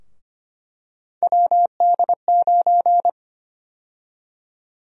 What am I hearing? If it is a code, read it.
Morse code, heard as a keyed tone, about 25 wpm, 700 Hz. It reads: WD9
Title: USA Call Sign Prefixes - Repeated 2x - 25wpm